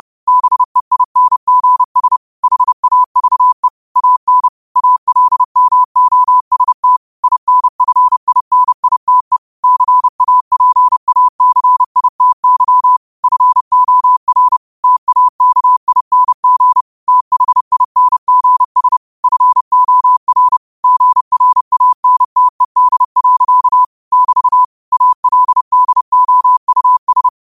Morse code, heard as a keyed tone, about 30 wpm, 1000 Hz. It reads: BEINGS HAVE AN ALMOST INFINITE CAPACITY FOR TAKING THINGS FOR GRANTED. <BT> ALDOUS